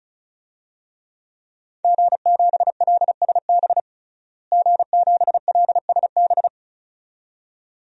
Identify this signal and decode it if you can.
Morse code, heard as a keyed tone, about 36 wpm, 700 Hz. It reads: G7LSB G7LSB
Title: DXCC-100 Call Signs - 5 Characters - Repeated 2x - 35wpm